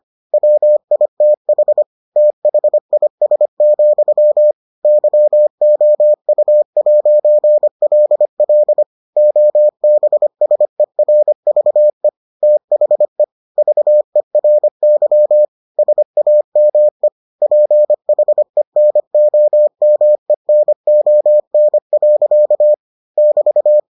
Morse code, heard as a keyed tone, about 25 wpm, 600 Hz. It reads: WITH THIS, YOU'LL OBSERVE THE VERY SAME PHENOMENON. <BT>